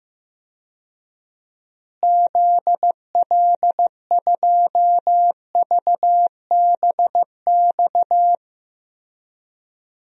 Morse code, heard as a keyed tone, about 15 wpm, 700 Hz. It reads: ZL2VBX
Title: DXCC-100 Call Signs - 6 Characters - Repeated 2x - 15wpm